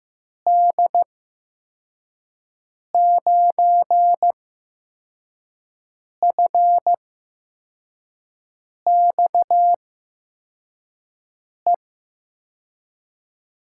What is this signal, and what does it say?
Morse code, heard as a keyed tone, about 15 wpm, spaced as for 4 wpm, 700 Hz. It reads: D9FXE
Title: USA Call Signs - 6 Characters - 8x Character Spacing - 15wpm